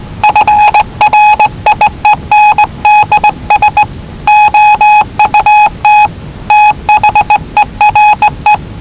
Morse code, heard as a keyed tone, about 19 wpm, 840 Hz. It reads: FRIENDS OUT THERE
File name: WellingtonRadioZlwCloseDownOnHfAndMf / 01-Zlw 4277khz 417.5 qrt until 0925z